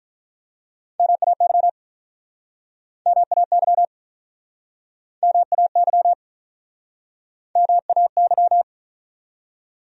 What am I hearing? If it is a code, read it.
Morse code, heard as a keyed tone, about 48 wpm, 700 Hz. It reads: MAY MAY MAY MAY